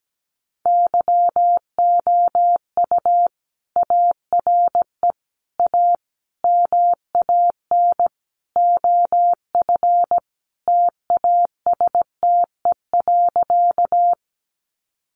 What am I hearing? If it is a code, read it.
Morse code, heard as a keyed tone, about 17 wpm, 700 Hz. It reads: YOU ARE A MAN OF TASTE.
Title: Sentences from Top 1100 Words 17wpm